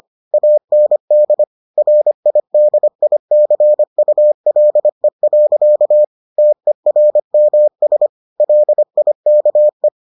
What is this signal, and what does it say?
Morse code, heard as a keyed tone, about 25 wpm, 600 Hz. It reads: AND RIDICULE. TERMS LIKE